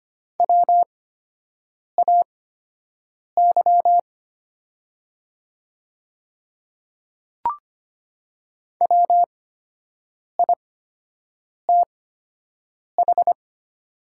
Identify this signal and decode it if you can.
Morse code, heard as a keyed tone, about 25 wpm, spaced as for 7 wpm, 700 Hz. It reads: WAY WITH
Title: 8x Character Spacing - Top 200 Words - 25wpm